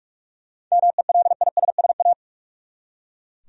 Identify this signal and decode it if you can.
Morse code, heard as a keyed tone, about 42 wpm, 700 Hz. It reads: MELISSA
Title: USA Common Names - 45wpm